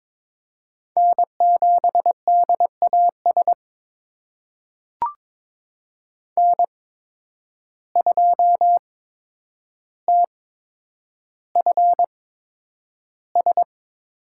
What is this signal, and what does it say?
Morse code, heard as a keyed tone, about 23 wpm, spaced as for 10 wpm, 700 Hz. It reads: N7DAS N2TFS
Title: USA Call Signs - 8x Character Spacing - 22wpm